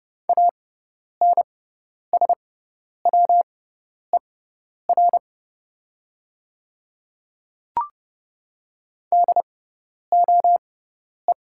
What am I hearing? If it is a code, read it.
Morse code, heard as a keyed tone, about 30 wpm, spaced as for 10 wpm, 700 Hz. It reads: ANSWER DOE
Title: 6x Character Spacing - Top 300 Words - 30wpm